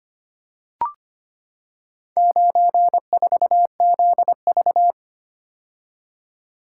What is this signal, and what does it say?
Morse code, heard as a keyed tone, about 25 wpm, 700 Hz. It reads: 94ZV
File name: practice / 25wpm / Four Letter-Number Combinations - 25wpm